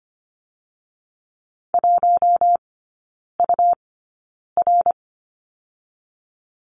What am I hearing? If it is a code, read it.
Morse code, heard as a keyed tone, about 25 wpm, 700 Hz. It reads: 1 U R
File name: Lesson 17e - Letter u - Three Characters 25wpm